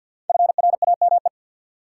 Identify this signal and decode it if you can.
Morse code, heard as a keyed tone, about 53 wpm, 700 Hz. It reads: FRAME
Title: Five Letter Words - 50wpm